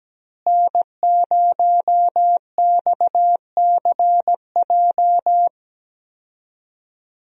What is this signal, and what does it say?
Morse code, heard as a keyed tone, about 17 wpm, 700 Hz. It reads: N0XCJ
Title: USA Call Signs - 5 Characters - Repeated 3x - 17wpm